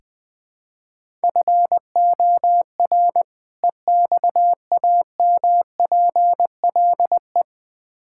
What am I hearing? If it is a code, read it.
Morse code, heard as a keyed tone, about 20 wpm, 700 Hz. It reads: FOR EXAMPLE
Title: Build A Phrase - 2-Word Phrases - 20wpm